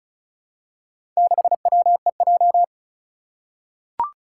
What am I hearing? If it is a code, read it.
Morse code, heard as a keyed tone, about 36 wpm, 700 Hz. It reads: 6WEJ
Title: Four Challenging Letter-Number Combinations 35wpm